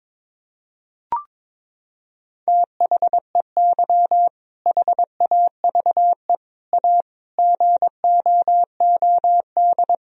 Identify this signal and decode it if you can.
Morse code, heard as a keyed tone, about 22 wpm, 700 Hz. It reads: THEY HAVE A GOOD
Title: Sentences from Top 1000 Words - Review Part 4 - 22wpm